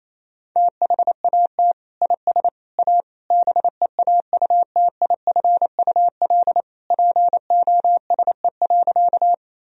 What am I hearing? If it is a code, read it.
Morse code, heard as a keyed tone, about 28 wpm, 700 Hz. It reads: THAT IS A BEAUTIFUL POSE.
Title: Sentences from Top 1000 Words - Review Part 2 - 28wpm